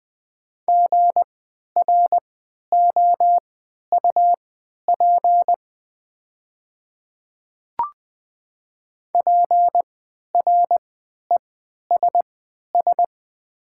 Morse code, heard as a keyed tone, about 20 wpm, spaced as for 11 wpm, 700 Hz. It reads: GROUP PRESS
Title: 3x Character Spacing - Top 300 Words - 20wpm